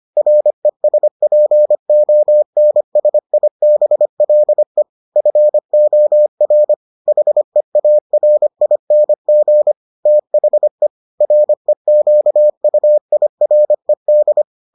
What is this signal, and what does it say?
Morse code, heard as a keyed tone, about 25 wpm, 600 Hz. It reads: RESPONSIBLE FOR HEARING THE REQUIRED